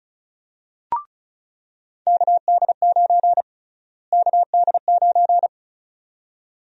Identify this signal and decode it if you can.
Morse code, heard as a keyed tone, about 34 wpm, 700 Hz. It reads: KD9 KD9
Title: USA Call Sign Prefixes - Repeated 2x - 35wpm